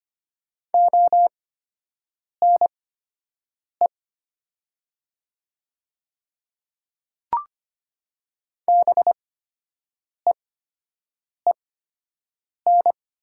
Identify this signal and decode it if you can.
Morse code, heard as a keyed tone, about 25 wpm, spaced as for 7 wpm, 700 Hz. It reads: ONE BEEN
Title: 8x Character Spacing - Top 200 Words - 25wpm